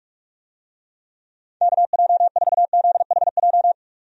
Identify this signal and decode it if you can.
Morse code, heard as a keyed tone, about 46 wpm, 700 Hz. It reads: KJ4ZHJ
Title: USA Call Signs - 6 Characters - 45wpm